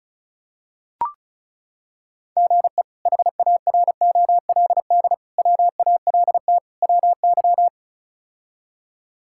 Morse code, heard as a keyed tone, about 34 wpm, 700 Hz. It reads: GE HAROLD WALT WY
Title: K1USN SST QSO Practice - 35wpm